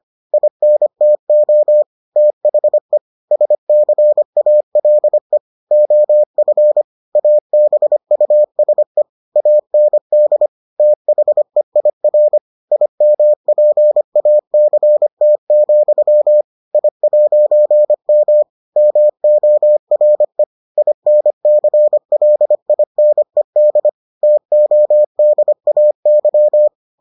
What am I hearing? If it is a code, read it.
Morse code, heard as a keyed tone, about 25 wpm, 600 Hz. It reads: INTO THE SCALE OF ABUSE AND THEIR IMPACT, I'M MORE INCLINED TODAY